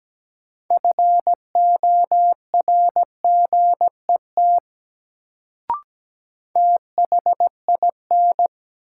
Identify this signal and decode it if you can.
Morse code, heard as a keyed tone, about 17 wpm, 700 Hz. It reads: FORGET THIN
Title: Sentences from Top 1200 Words - New Word Drill - 17wpm